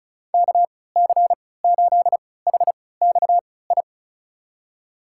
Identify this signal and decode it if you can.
Morse code, heard as a keyed tone, about 36 wpm, spaced as for 20 wpm, 700 Hz. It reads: KC8HXI
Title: USA Call Signs - 6 Characters - 3x Character Spacing - 35wpm